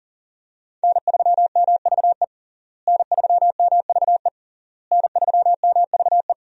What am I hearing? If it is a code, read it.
Morse code, heard as a keyed tone, about 40 wpm, 700 Hz. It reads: N3MVE N3MVE N3MVE
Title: USA Call Signs - 5 Characters - Repeated 3x - 40wpm